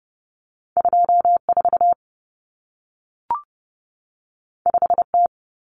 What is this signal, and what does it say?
Morse code, heard as a keyed tone, about 30 wpm, 700 Hz. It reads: 24 5T
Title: Common Cut Numbers - 2-Digit 30wpm